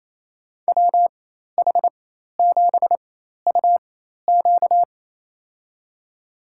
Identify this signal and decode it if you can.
Morse code, heard as a keyed tone, about 28 wpm, spaced as for 13 wpm, 700 Hz. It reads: WH7UQ
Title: USA Call Signs - 5 Characters - 4x Character Spacing - 28wpm